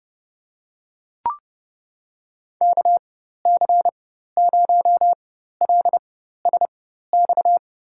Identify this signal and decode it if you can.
Morse code, heard as a keyed tone, about 30 wpm, spaced as for 14 wpm, 700 Hz. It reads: KC0LSX
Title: USA Call Signs - 6 Characters - 4x Character Spacing - 30wpm